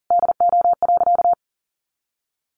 Morse code, heard as a keyed tone, about 40 wpm, 700 Hz. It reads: DO.
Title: Sentences from Top 100 Words 40wpm